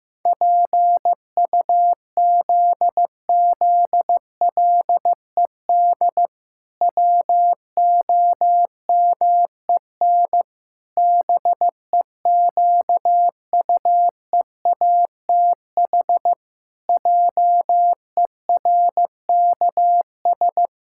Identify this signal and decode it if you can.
Morse code, heard as a keyed tone, about 15 wpm, 700 Hz. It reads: PUZZLED WOMEN BEQUEATH JERKS
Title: Pangram Challenge v2 - 15wpm